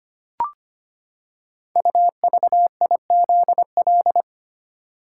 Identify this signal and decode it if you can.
Morse code, heard as a keyed tone, about 25 wpm, 700 Hz. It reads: UVIZL
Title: Code Groups - 5 Letters - 25wpm